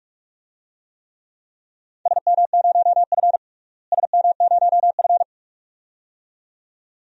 Morse code, heard as a keyed tone, about 46 wpm, 700 Hz. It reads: SM0F SM0F
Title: DXCC-100 Call Signs - 4 Characters - Repeated 2x - 45wpm